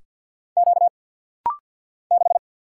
Morse code, heard as a keyed tone, about 50 wpm, 700 Hz. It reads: Q 6
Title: Single Letter-Number Alt - Rapid-Fire 50wpm